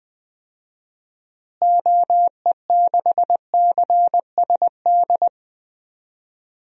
Morse code, heard as a keyed tone, about 20 wpm, 700 Hz. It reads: OE6CSD
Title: DXCC-100 Call Signs - 6 Characters - Repeated 2x - 20wpm